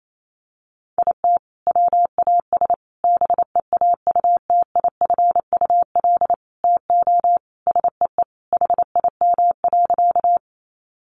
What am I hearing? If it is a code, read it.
Morse code, heard as a keyed tone, about 28 wpm, 700 Hz. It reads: IT WAS BEAUTIFUL TO SEE HIM.
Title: Sentences from Top 900 Words 28wpm